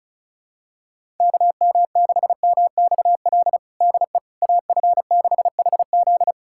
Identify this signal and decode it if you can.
Morse code, heard as a keyed tone, about 34 wpm, 700 Hz. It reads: KM6MXL DE AF6HZ
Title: USA Callsign DE Callsign 35wpm